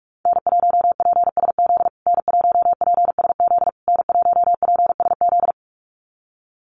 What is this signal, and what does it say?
Morse code, heard as a keyed tone, about 44 wpm, 700 Hz. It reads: N1PSZ N1PSZ N1PSZ
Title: USA Call Signs Repeated 3x 45wpm